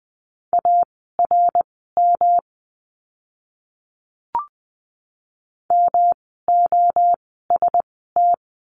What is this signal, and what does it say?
Morse code, heard as a keyed tone, about 20 wpm, spaced as for 14 wpm, 700 Hz. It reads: ARM MOST
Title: 108 Lesson 18f - Letter m - Words 20wpm